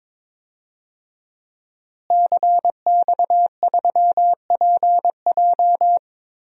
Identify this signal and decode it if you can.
Morse code, heard as a keyed tone, about 21 wpm, 700 Hz. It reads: CX3PJ
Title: DXCC-100 Call Signs - 5 Characters - Repeated 3x - 22wpm